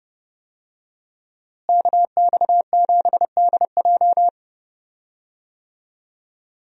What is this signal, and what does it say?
Morse code, heard as a keyed tone, about 30 wpm, 700 Hz. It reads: KX7DJ
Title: USA Call Signs - 5 Characters - Repeated 3x - 30wpm